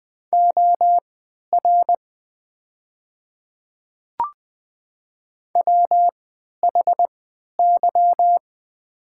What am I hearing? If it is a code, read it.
Morse code, heard as a keyed tone, about 20 wpm, spaced as for 11 wpm, 700 Hz. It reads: OR WHY
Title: 3x Character Spacing - Top 300 Words - 20wpm